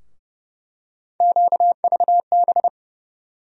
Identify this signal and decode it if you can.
Morse code, heard as a keyed tone, about 32 wpm, 700 Hz. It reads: QVB
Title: USA Call Sign Postfixes - 3 Letters 30wpm